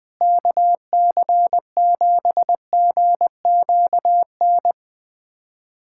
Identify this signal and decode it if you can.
Morse code, heard as a keyed tone, about 20 wpm, 700 Hz. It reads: KC7GQN
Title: USA Callsign DE Callsign 20wpm